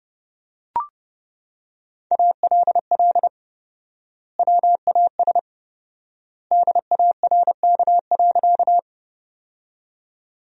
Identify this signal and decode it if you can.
Morse code, heard as a keyed tone, about 32 wpm, 700 Hz. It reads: ALL WAS DARK.
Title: 4x Word-Spacing - Sentences from Top 500 Words - 30wpm